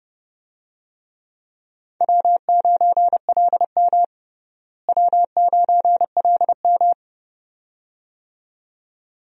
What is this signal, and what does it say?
Morse code, heard as a keyed tone, about 30 wpm, 700 Hz. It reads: W9LM W9LM
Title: USA Call Signs - 4 Characters - Repeated 2x - 30wpm